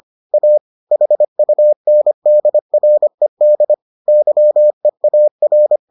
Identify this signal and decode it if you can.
Morse code, heard as a keyed tone, about 25 wpm, 600 Hz. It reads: A HUNDRED YEAR